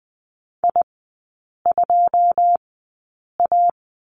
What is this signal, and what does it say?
Morse code, heard as a keyed tone, about 20 wpm, spaced as for 8 wpm, 690 Hz. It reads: I2A
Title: 081 Lesson 14e - Number 2 - Three Characters 20wpm